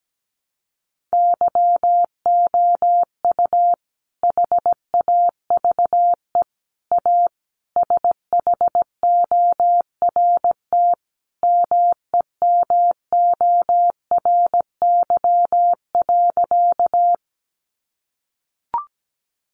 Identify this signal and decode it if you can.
Morse code, heard as a keyed tone, about 17 wpm, 700 Hz. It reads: YOU HAVE A SHORT MEMORY.
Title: Sentences from Top 1300 Words 17wpm